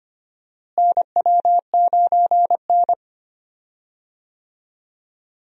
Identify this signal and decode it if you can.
Morse code, heard as a keyed tone, about 25 wpm, 700 Hz. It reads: NW9N
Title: USA Call Signs - 4 Characters - Repeated 3x - 25wpm